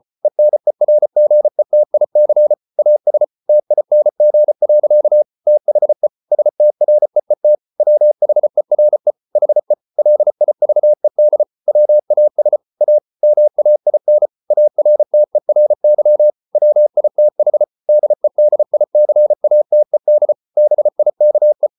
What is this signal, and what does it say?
Morse code, heard as a keyed tone, about 34 wpm, 600 Hz. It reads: ENERGETIC AS TING. THE STREET WHERE HE LIVED WAS A MAIN ARTERY WITH DEDICATED BIKE